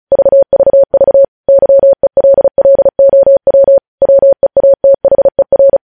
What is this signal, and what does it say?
Morse code, heard as a keyed tone, about 36 wpm, 550 Hz. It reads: VVV YELLOW WEATHER
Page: news headlines in morse code at 35 wpm